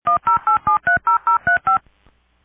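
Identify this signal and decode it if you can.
DTMF, heard as telephone touch tones, tones of about 120 ms. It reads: 108730035
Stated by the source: Internet dial-up tone.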